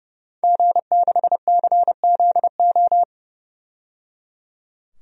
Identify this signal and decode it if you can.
Morse code, heard as a keyed tone, about 31 wpm, 700 Hz. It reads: G6CZO
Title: DXCC-100 Call Signs - 5 Characters - 30wpm